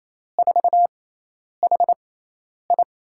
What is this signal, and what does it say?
Morse code, heard as a keyed tone, about 28 wpm, spaced as for 10 wpm, 700 Hz. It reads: 4HI
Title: USA Call Signs - 5 Characters - 6x Character Spacing - 28wpm